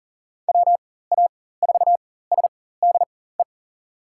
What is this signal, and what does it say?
Morse code, heard as a keyed tone, about 37 wpm, spaced as for 18 wpm, 700 Hz. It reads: WA4SDE
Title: USA Call Signs - 6 Characters - 4x Character Spacing - 40wpm